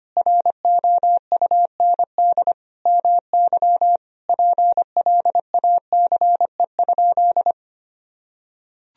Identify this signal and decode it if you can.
Morse code, heard as a keyed tone, about 25 wpm, 700 Hz. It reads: ROUND MY PLACE?
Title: Sentences from Top 200 Words and Abbreviations 25wpm